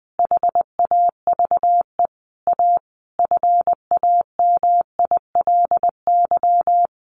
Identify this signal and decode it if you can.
Morse code, heard as a keyed tone, about 20 wpm, 700 Hz. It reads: HAVE A FAMILY
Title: Sentences from Top 1500 Words 20wpm